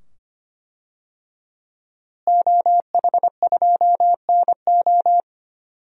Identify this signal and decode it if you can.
Morse code, heard as a keyed tone, about 25 wpm, 700 Hz. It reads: OH2NO
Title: DXCC-100 Call Signs - 5 Characters - Repeated 2x - 25wpm